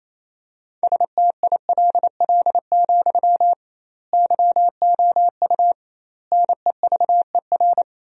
Morse code, heard as a keyed tone, about 28 wpm, 700 Hz. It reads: STILL, YOU NEVER
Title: 2x Word-Spacing - Sentences from Top 300 Words - 28wpm